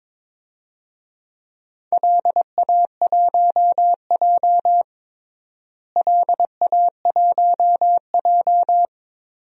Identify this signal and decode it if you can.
Morse code, heard as a keyed tone, about 22 wpm, 700 Hz. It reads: LA1J LA1J
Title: DXCC-100 Call Signs - 4 Characters - Repeated 3x - 22wpm